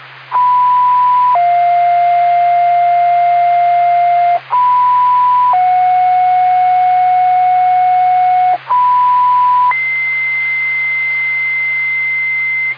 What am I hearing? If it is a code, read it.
Two-tone page, heard as a one-second tone, then a three-second tone, about 990 then 700 Hz, about 990 then 730 Hz, about 990 then 2050 Hz. It is heard from 0.3 to 12.7 s.